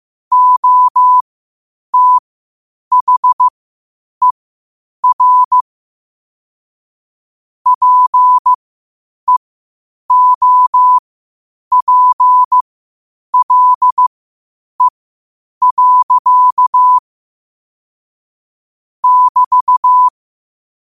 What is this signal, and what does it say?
Morse code, heard as a keyed tone, about 15 wpm, spaced as for 8 wpm, 1000 Hz. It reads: OTHER PEOPLE. <BT>
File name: qotdfiles / 8wpm